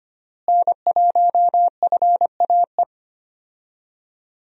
Morse code, heard as a keyed tone, about 25 wpm, 700 Hz. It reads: N1FAE
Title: USA Call Signs - 5 Characters - Repeated 3x - 25wpm